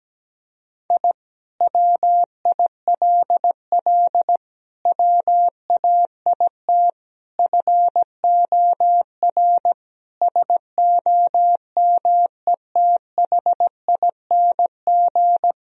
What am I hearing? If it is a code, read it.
Morse code, heard as a keyed tone, about 17 wpm, 700 Hz. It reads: I WILL WAIT FOR SOMETHING
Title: Sentences from Top 1000 Words - Review Part 1 - 17wpm